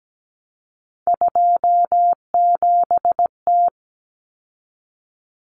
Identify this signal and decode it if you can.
Morse code, heard as a keyed tone, about 17 wpm, 690 Hz. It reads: 27T
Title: Common Cut Numbers - 3-Digit 17wpm